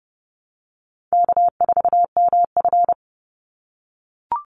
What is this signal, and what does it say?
Morse code, heard as a keyed tone, about 30 wpm, 700 Hz. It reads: K4MF
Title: NA SOTA Chaser Callsigns 30wpm